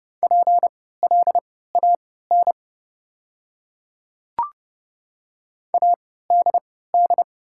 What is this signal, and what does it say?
Morse code, heard as a keyed tone, about 30 wpm, spaced as for 17 wpm, 700 Hz. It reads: PLAN ADD
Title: 3x Character Spacing - Top 500 Words - 30wpm